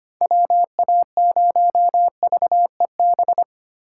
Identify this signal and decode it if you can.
Morse code, heard as a keyed tone, about 25 wpm, 700 Hz. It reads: WA0VEB
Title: USA Call Signs - 6 Characters - 4x Character Spacing - 25wpm